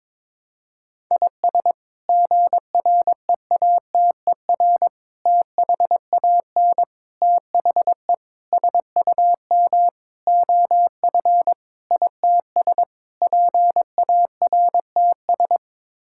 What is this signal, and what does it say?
Morse code, heard as a keyed tone, about 22 wpm, 700 Hz. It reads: IS GREATER THAN THE SUM OF ITS PARTS
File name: practice / 22wpm / Sets of 8 Words - 22wpm